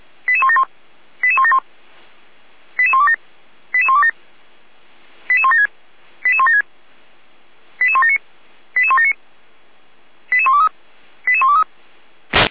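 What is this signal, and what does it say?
Five-tone selective call (ZVEI), heard as quick runs of five tones, tones of about 70 ms. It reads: 88171 88171 88127 88127 88176 88176 88179 88179 88123 88123
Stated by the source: ZVEI five-tone selective call received on an SDR, archived by Feuerwehr Orlamünde fire brigade.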